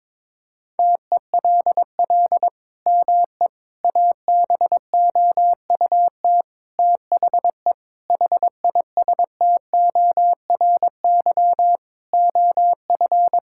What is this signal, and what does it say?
Morse code, heard as a keyed tone, about 21 wpm, 700 Hz. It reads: TELL ME ABOUT THE HISTORY OF
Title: Sentences from Top 1000 Words - Review Part 6 - 22wpm